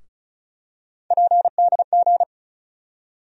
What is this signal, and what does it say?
Morse code, heard as a keyed tone, about 36 wpm, 700 Hz. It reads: PDG
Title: USA Call Sign Postfixes - 3 Letters 35wpm